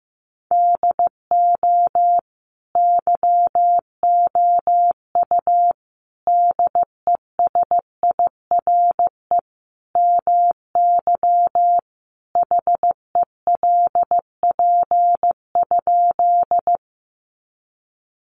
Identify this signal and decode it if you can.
Morse code, heard as a keyed tone, about 15 wpm, 700 Hz. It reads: DO YOU DESIRE MY HELP?
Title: Sentences from Top 1100 Words 15wpm